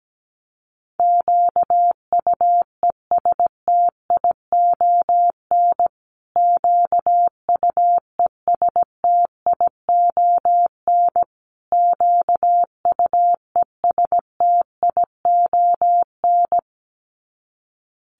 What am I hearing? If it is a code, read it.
Morse code, heard as a keyed tone, about 17 wpm, 700 Hz. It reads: QUESTION QUESTION QUESTION
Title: Top 500 Words Repeated 17wpm